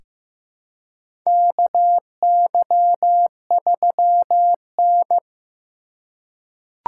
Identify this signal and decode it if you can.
Morse code, heard as a keyed tone, about 15 wpm, 700 Hz. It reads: KY3N
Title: USA Call Signs - 4 Characters - 4x Character Spacing - 15wpm